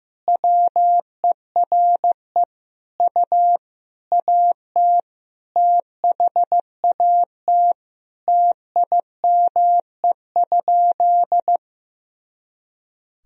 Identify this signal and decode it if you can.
Morse code, heard as a keyed tone, about 15 wpm, 700 Hz. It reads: WERE U AT THAT TIME?
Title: Sentences from Top 200 Words and Abbreviations 15wpm